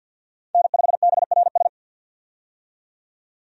Morse code, heard as a keyed tone, about 50 wpm, 700 Hz. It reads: N5BRS
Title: USA Call Signs - 5 Characters - 50wpm